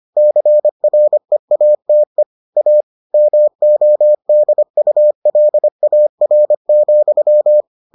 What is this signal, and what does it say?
Morse code, heard as a keyed tone, about 25 wpm, 600 Hz. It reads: CREATE A MODULAR,